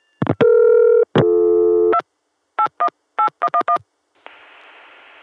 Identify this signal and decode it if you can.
DTMF, heard as telephone touch tones, tones of about 80 ms. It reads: 2 818111